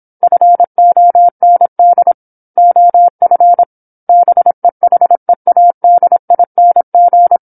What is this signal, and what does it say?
Morse code, heard as a keyed tone, about 27 wpm, 700 Hz. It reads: FOND OF BEHEADING
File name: 08-Chapter-08 26 WPM